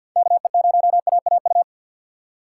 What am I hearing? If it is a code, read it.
Morse code, heard as a keyed tone, about 53 wpm, 700 Hz. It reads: KE0AAU
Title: USA Call Signs - 4x Character Spacing - 50wpm